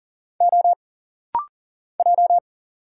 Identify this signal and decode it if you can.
Morse code, heard as a keyed tone, about 40 wpm, 700 Hz. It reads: O J
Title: Single Letters - Rapid-Fire 40wpm